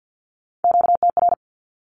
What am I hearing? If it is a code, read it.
Morse code, heard as a keyed tone, about 50 wpm, 700 Hz. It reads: QTR
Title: Q-Codes 50wpm